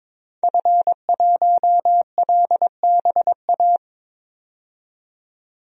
Morse code, heard as a keyed tone, about 22 wpm, 700 Hz. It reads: F1LBA
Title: DXCC-100 Call Signs - 5 Characters - 22wpm